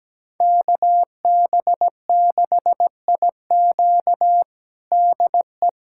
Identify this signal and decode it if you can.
Morse code, heard as a keyed tone, about 17 wpm, 700 Hz. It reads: KB6IQ DE
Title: USA Callsign DE Callsign - 5 Characters - 17wpm